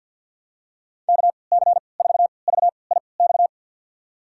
Morse code, heard as a keyed tone, about 52 wpm, spaced as for 29 wpm, 700 Hz. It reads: KC4VIX
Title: USA Call Signs - 6 Characters - 3x Character Spacing - 50wpm